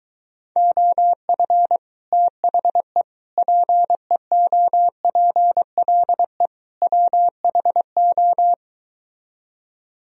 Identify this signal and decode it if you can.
Morse code, heard as a keyed tone, about 23 wpm, 700 Hz. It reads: OF THE PEOPLE WHO